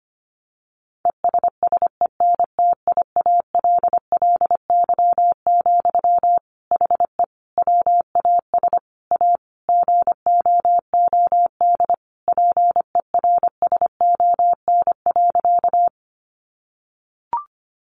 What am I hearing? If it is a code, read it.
Morse code, heard as a keyed tone, about 25 wpm, 700 Hz. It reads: ESSENTIALLY, HE WAS A GOOD PERSON.
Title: Sentences from Top 1300 Words 25wpm